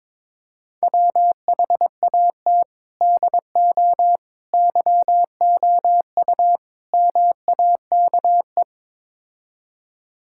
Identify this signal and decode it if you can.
Morse code, heard as a keyed tone, about 22 wpm, 700 Hz. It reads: WHAT DO YOU MAKE
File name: practice / 22wpm / Build A Phrase - 5-Word Phrases - 22wpm